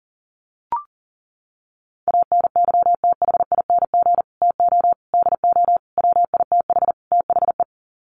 Morse code, heard as a keyed tone, about 37 wpm, 700 Hz. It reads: ANYTHING TO DO WITH THE
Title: Sets of 5 Words Encore 40wpm